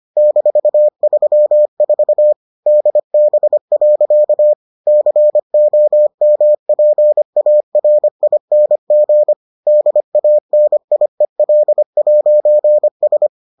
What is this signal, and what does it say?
Morse code, heard as a keyed tone, about 25 wpm, 600 Hz. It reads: -34 DB. COMPARING DANIEL'S